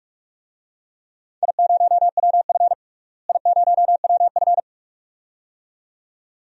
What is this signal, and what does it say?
Morse code, heard as a keyed tone, about 47 wpm, 700 Hz. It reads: I0WF I0WF
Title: DXCC-100 Call Signs - 4 Characters - Repeated 2x - 45wpm